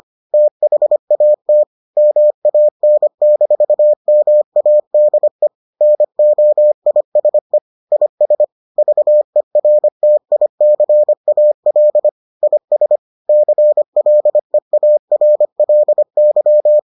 Morse code, heard as a keyed tone, about 25 wpm, 600 Hz. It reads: THAT MAN-MADE NOISE IS VERTICAL IS CLEARLY